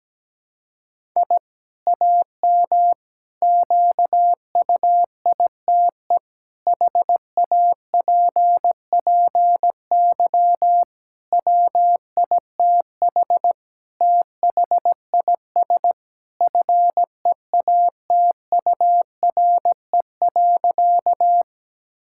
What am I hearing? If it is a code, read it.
Morse code, heard as a keyed tone, about 17 wpm, 700 Hz. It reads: I AM QUITE HAPPY WITH THIS FEATURE.
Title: Sentences from Top 1000 Words - Review Part 5 - 17wpm